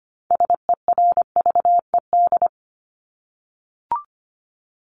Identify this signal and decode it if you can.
Morse code, heard as a keyed tone, about 25 wpm, 700 Hz. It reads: SERVED
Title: ED Suffix - Top 100 Words with Base Words 25wpm